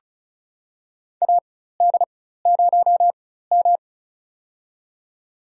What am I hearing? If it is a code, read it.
Morse code, heard as a keyed tone, about 36 wpm, spaced as for 16 wpm, 700 Hz. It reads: AD0M
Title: USA Call Signs - 4 Characters - 4x Character Spacing - 35wpm